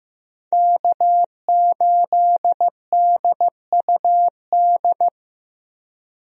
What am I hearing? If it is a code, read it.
Morse code, heard as a keyed tone, about 15 wpm, 700 Hz. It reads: K8DUD
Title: USA Call Signs - 5 Characters - 15wpm